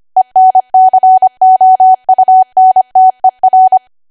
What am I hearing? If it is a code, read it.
Morse code, heard as a keyed tone, about 25 wpm, 750 Hz. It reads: ENCOUNTER
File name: Archive / 25 WPM / 25WPM